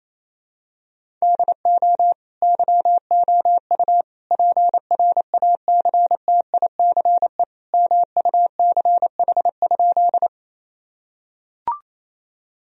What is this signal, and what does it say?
Morse code, heard as a keyed tone, about 27 wpm, 700 Hz. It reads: DO YOU PRACTICE MUCH?
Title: Sentences from Top 1000 Words - Review Part 4 - 28wpm